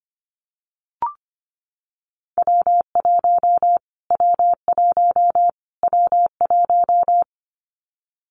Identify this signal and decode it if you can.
Morse code, heard as a keyed tone, about 25 wpm, 700 Hz. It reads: W1 W1 W1
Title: Callsign Prefixes for Top 25 DXCC - Repeated 3x - 25wpm